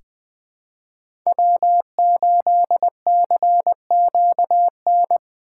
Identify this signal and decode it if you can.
Morse code, heard as a keyed tone, about 20 wpm, 700 Hz. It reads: W8CQN
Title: USA Call Signs - 5 Characters - 3x Character Spacing - 20wpm